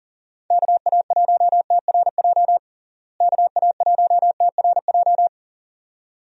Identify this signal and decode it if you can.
Morse code, heard as a keyed tone, about 40 wpm, 700 Hz. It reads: KA1TRJ KA1TRJ
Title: USA Call Signs - 6 Characters - Repeated 2x - 40wpm